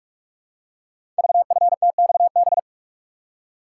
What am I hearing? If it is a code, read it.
Morse code, heard as a keyed tone, about 44 wpm, 700 Hz. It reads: VFTXB
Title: Code Groups - 5 Letters - 45wpm